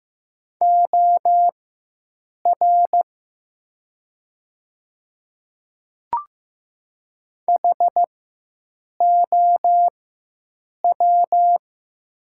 Morse code, heard as a keyed tone, about 15 wpm, spaced as for 7 wpm, 700 Hz. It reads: OR HOW
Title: 4x Character Spacing - Top 300 Words - 15wpm